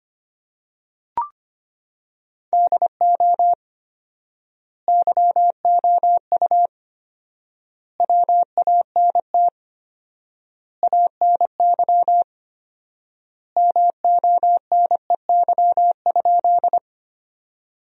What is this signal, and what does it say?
Morse code, heard as a keyed tone, about 25 wpm, 700 Hz. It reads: DO YOU WANT ANY MONEY?
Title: 4x Word-Spacing - Sentences from Top 400 Words - 25wpm